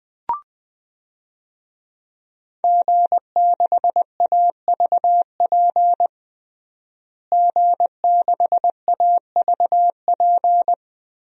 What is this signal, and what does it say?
Morse code, heard as a keyed tone, about 20 wpm, 700 Hz. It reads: G6AVP G6AVP
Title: DXCC-100 Call Signs - 5 Characters - Repeated 3x - 20wpm